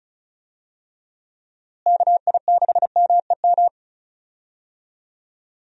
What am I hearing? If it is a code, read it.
Morse code, heard as a keyed tone, about 36 wpm, 700 Hz. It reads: KI6MEM
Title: USA Call Signs - 6 Characters - Repeated 3x - 35wpm